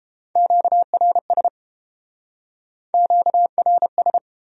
Morse code, heard as a keyed tone, about 33 wpm, 700 Hz. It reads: QRS QRS